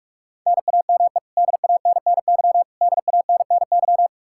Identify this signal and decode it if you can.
Morse code, heard as a keyed tone, about 44 wpm, 700 Hz. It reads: NAME DANNY DANNY
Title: Practice QSO - Exchange 1 - Beginning - 45wpm